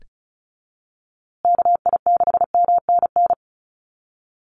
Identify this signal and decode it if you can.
Morse code, heard as a keyed tone, about 34 wpm, 700 Hz. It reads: KI6MNN
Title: USA Call Signs Repeated 2x 35wpm